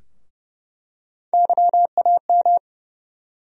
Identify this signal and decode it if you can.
Morse code, heard as a keyed tone, about 31 wpm, 700 Hz. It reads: YAM
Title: USA Call Sign Postfixes - 3 Letters 30wpm